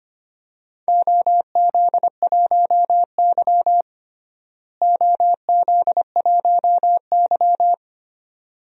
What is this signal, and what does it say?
Morse code, heard as a keyed tone, about 25 wpm, 700 Hz. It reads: OZ1Y OZ1Y
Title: DXCC-100 Call Signs - 4 Characters - Repeated 2x - 25wpm